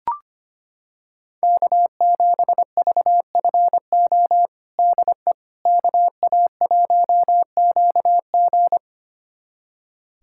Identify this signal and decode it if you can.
Morse code, heard as a keyed tone, about 25 wpm, 700 Hz. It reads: K7VFO DE KA1QG
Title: USA Callsign DE Callsign - 5 Characters - 25wpm